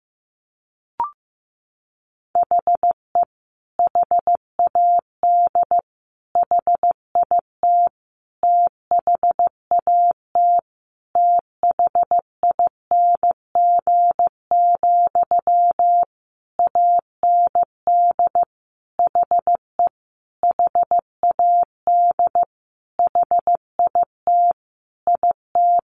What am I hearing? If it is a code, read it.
Morse code, heard as a keyed tone, about 14 wpm, 700 Hz. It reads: HE HAD HIT THAT THING, AND HE HAD HIT IT
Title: Sentences from Top 1200 Words 15wpm